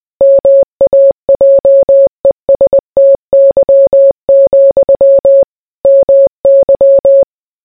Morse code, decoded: MAJESTY, MY